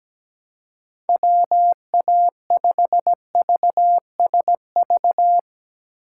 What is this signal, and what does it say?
Morse code, heard as a keyed tone, about 18 wpm, 700 Hz. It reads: WA5VSV
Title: Speed Racing - USA Call Signs 17wpm